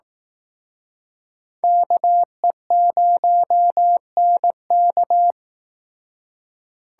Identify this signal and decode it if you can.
Morse code, heard as a keyed tone, about 18 wpm, 700 Hz. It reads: KE0NK
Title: Speed Racing - USA 2x2 Call Signs 15wpm